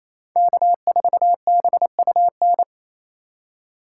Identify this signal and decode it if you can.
Morse code, heard as a keyed tone, about 28 wpm, 700 Hz. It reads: K4BUN
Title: USA Call Signs - 3x Character Spacing - 28wpm